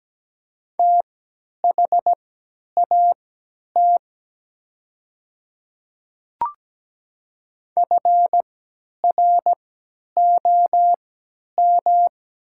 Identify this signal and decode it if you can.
Morse code, heard as a keyed tone, about 17 wpm, spaced as for 10 wpm, 700 Hz. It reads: THAT FROM
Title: 3x Character Spacing - Top 200 Words - 17wpm